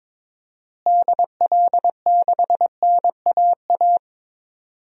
Morse code, heard as a keyed tone, about 22 wpm, 700 Hz. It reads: DL6NAA
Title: DXCC-100 Call Signs - 6 Characters - Repeated 3x - 22wpm